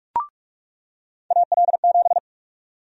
Morse code, heard as a keyed tone, about 46 wpm, 700 Hz. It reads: AL7
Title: USA Call Sign Prefixes - 45wpm